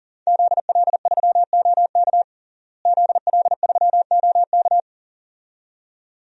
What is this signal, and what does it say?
Morse code, heard as a keyed tone, about 37 wpm, 700 Hz. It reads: ZL3OK ZL3OK
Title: DXCC-100 Call Signs - 5 Characters - Repeated 2x - 40wpm